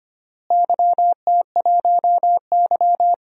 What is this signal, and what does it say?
Morse code, heard as a keyed tone, about 25 wpm, 700 Hz. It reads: YT1Y
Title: DXCC-100 Call Signs - 4 Characters - Repeated 2x - 25wpm